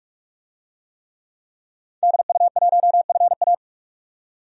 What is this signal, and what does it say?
Morse code, heard as a keyed tone, about 41 wpm, 700 Hz. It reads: DU1FA
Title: DXCC-100 Call Signs - 5 Characters - 45wpm